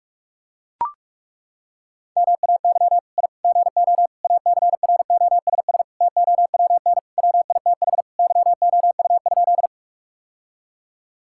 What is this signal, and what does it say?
Morse code, heard as a keyed tone, about 42 wpm, 700 Hz. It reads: MAY I GO ACROSS TOWN WITH YOU?
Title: Sentences from Top 1000 Words - Review Part 5 - 45wpm